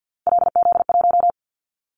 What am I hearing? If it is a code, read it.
Morse code, heard as a keyed tone, about 50 wpm, 700 Hz. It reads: LZ1
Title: Callsign Prefixes for Top 50 DXCC - 50wpm